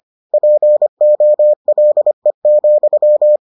Morse code, heard as a keyed tone, about 25 wpm, 600 Hz. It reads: POLE,